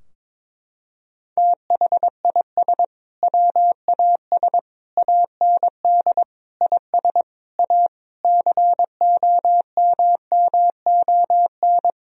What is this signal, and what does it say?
Morse code, heard as a keyed tone, about 22 wpm, 700 Hz. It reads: THIS WAS AND IS A COMMON
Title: Sentences from Top 1000 Words - Review Part 4 - 22wpm